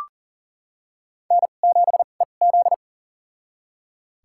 Morse code, heard as a keyed tone, about 40 wpm, spaced as for 29 wpm, 700 Hz. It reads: N7EZ
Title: USA Call Signs - 4 Characters - 2x Character Spacing - 40wpm